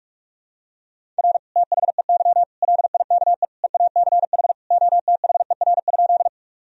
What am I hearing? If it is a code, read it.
Morse code, heard as a keyed tone, about 46 wpm, 700 Hz. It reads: R THEY LIKE EACH OTHER?
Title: Sentences from Top 200 Words and Abbreviations 45wpm